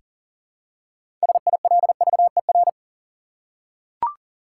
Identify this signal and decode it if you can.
Morse code, heard as a keyed tone, about 40 wpm, 700 Hz. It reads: SILVER
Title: Six Letter Words - 40wpm